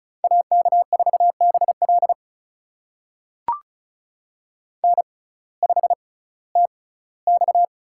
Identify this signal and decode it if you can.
Morse code, heard as a keyed tone, about 34 wpm, spaced as for 19 wpm, 700 Hz. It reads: AK4BL N5TX
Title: USA Call Signs - 5 Characters - 6x Character Spacing - 35wpm